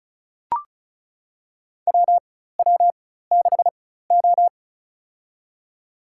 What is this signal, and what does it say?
Morse code, heard as a keyed tone, about 34 wpm, spaced as for 16 wpm, 700 Hz. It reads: WW6O
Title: USA Call Signs - 4 Characters - 4x Character Spacing - 35wpm